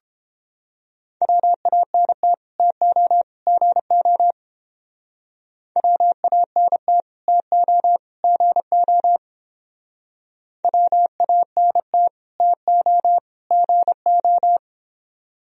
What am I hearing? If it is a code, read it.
Morse code, heard as a keyed tone, about 30 wpm, 700 Hz. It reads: WANT TO GO WANT TO GO WANT TO GO